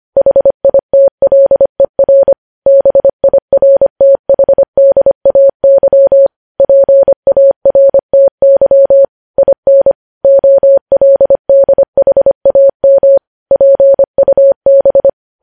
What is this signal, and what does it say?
Morse code, heard as a keyed tone, about 25 wpm, 550 Hz. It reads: HITLER BIRTHDAY PARTY IN OLDHAM PUB
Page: news headlines in morse code at 25 wpm